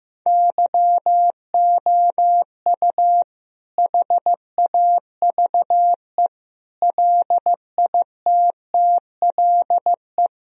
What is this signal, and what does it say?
Morse code, heard as a keyed tone, about 15 wpm, 700 Hz. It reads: YOU HAVE LITTLE
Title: Sentences from Top 1000 Words - Review Part 1 - 15wpm